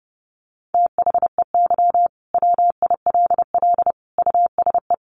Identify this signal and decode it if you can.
Morse code, heard as a keyed tone, about 30 wpm, 700 Hz. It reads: THEY WILL USE
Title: Sentences from Top 1500 Words 30wpm